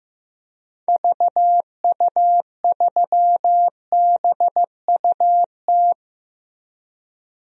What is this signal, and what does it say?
Morse code, heard as a keyed tone, about 15 wpm, 700 Hz. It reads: VU3BUT
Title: DXCC-100 Call Signs - 6 Characters - 15wpm